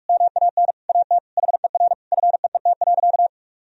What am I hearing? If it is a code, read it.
Morse code, heard as a keyed tone, about 46 wpm, 700 Hz. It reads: MAN AT HER FEET.
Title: Sentences from Top 1000 Words - Review Part 1 - 45wpm